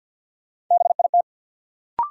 Morse code, heard as a keyed tone, about 53 wpm, 700 Hz. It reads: BIT
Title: Sets of 3 Words - New Word Drill - 50wpm